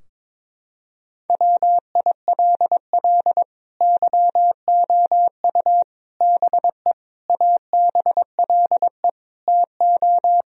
Morse code, heard as a keyed tone, about 22 wpm, 700 Hz. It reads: WILL YOU BE ABLE TO
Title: Sentences from Top 1000 Words - Review Part 2 - 22wpm